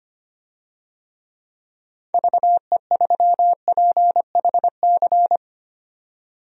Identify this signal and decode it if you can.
Morse code, heard as a keyed tone, about 25 wpm, 700 Hz. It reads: VE3PHC
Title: DXCC-100 Call Signs - 6 Characters - 25wpm